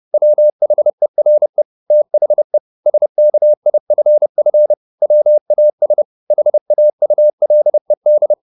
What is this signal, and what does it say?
Morse code, heard as a keyed tone, about 30 wpm, 600 Hz. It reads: WHERE THE SKIFF WAS HAULED
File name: the old man and the sea / the old man and the sea 1 30wpm